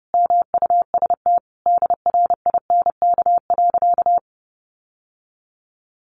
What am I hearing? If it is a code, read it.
Morse code, heard as a keyed tone, about 30 wpm, 700 Hz. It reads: MUST DRINK.
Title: Sentences from Top 1500 Words 30wpm